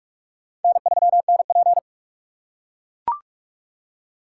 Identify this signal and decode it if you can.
Morse code, heard as a keyed tone, about 41 wpm, 700 Hz. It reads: N3NP
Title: USA Call Signs - 4 Characters - 2x Character Spacing - 45wpm